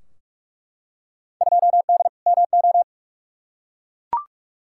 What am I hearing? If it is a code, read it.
Morse code, heard as a keyed tone, about 47 wpm, 700 Hz. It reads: 2D MO
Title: Field Day Exchanges - 45wpm